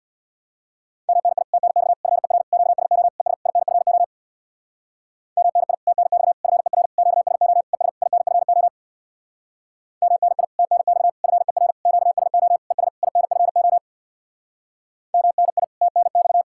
Morse code, heard as a keyed tone, about 66 wpm, 700 Hz. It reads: MNI TNX FER QSO ES INFO MNI TNX FER QSO ES INFO MNI TNX FER QSO ES INFO MNI TNX